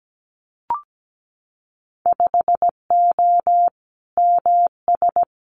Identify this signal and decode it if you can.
Morse code, heard as a keyed tone, about 17 wpm, 700 Hz. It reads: 5O MS
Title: Winter Field Day Exchanges Encore 17wpm